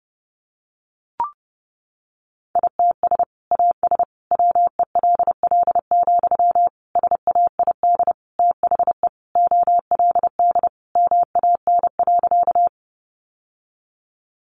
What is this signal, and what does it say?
Morse code, heard as a keyed tone, about 30 wpm, 700 Hz. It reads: ITS AS WELL, SAID THE OLD MAN.
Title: Sentences from Top 600 Words 30wpm